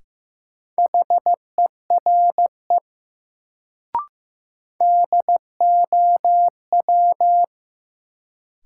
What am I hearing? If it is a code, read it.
Morse code, heard as a keyed tone, about 15 wpm, 700 Hz. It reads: HERE DOW
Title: Sets of 5 Words - New Word Drill - 15wpm